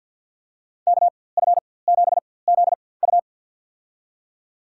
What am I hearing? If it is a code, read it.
Morse code, heard as a keyed tone, about 53 wpm, spaced as for 24 wpm, 700 Hz. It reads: KF7ZU
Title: USA Call Signs - 5 Characters - 4x Character Spacing - 50wpm